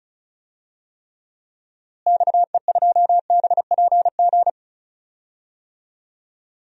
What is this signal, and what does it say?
Morse code, heard as a keyed tone, about 36 wpm, 700 Hz. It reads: XE2BPG
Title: DXCC-100 Call Signs - 6 Characters - Repeated 3x - 35wpm